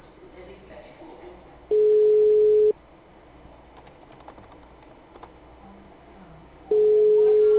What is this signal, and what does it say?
Ringback tone (Europe, 425 Hz), with speech.